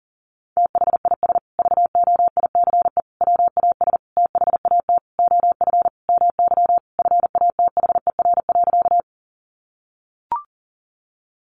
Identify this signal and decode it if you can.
Morse code, heard as a keyed tone, about 40 wpm, 700 Hz. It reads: THIS VOICE WAS THAT OF MY FATHER.
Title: Sentences from Top 500 Words 40wpm